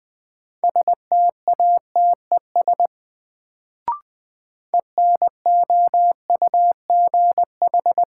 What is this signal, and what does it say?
Morse code, heard as a keyed tone, about 20 wpm, 700 Hz. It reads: STATES ENOUGH
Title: Sets of 2 Words - New Word Drill - 20wpm